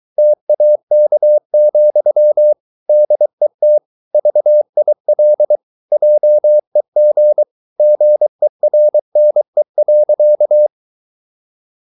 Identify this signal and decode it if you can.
Morse code, heard as a keyed tone, about 23 wpm, 600 Hz. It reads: TAK, DET VIL JEG GERNE.
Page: Ferie 23 wpm